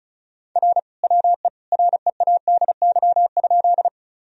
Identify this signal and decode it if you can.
Morse code, heard as a keyed tone, about 36 wpm, 700 Hz. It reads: R WE READY?
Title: Sentences from Top 400 Words and Abbreviations 35wpm